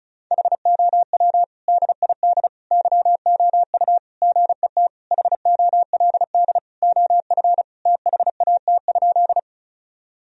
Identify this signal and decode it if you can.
Morse code, heard as a keyed tone, about 34 wpm, 700 Hz. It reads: HOW DID YOU GET HOLD OF THAT?
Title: Sentences from Top 1000 Words - Review Part 2 - 35wpm